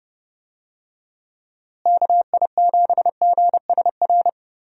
Morse code, heard as a keyed tone, about 31 wpm, 700 Hz. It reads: KI7GSR
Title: USA Call Signs - 6 Characters - Repeated 3x - 30wpm